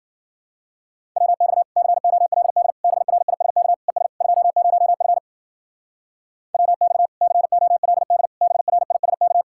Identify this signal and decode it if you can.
Morse code, heard as a keyed tone, about 50 wpm, 700 Hz. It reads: WX COLD BRISK ES 20F WX COLD BRISK